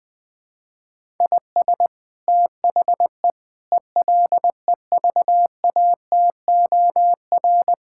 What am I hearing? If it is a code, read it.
Morse code, heard as a keyed tone, about 20 wpm, 700 Hz. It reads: IS THE ELEVATOR
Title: Sentences from Top 1600 Words 20wpm